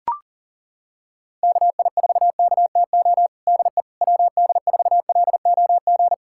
Morse code, heard as a keyed tone, about 37 wpm, 700 Hz. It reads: KI4KTO DE WD4LOG
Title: USA Callsign DE Callsign - 6 Characters - 40wpm